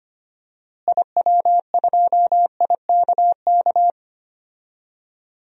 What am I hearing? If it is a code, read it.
Morse code, heard as a keyed tone, about 25 wpm, 700 Hz. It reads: IW2IKK
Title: DXCC-100 Call Signs - 6 Characters - Repeated 2x - 25wpm